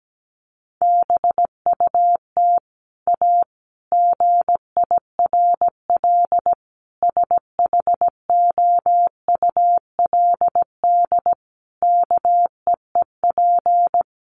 Morse code, heard as a keyed tone, about 17 wpm, 700 Hz. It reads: BUT A GIRL SHOULD KEEP
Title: Sentences from Top 1200 Words 17wpm